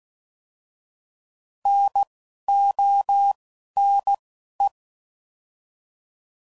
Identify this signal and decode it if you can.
Morse code, heard as a keyed tone, about 16 wpm, spaced as for 11 wpm, 800 Hz. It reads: NONE